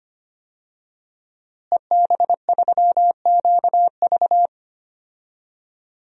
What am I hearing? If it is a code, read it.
Morse code, heard as a keyed tone, about 25 wpm, 700 Hz. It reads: EB3QV
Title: DXCC-100 Call Signs - 5 Characters - Repeated 3x - 25wpm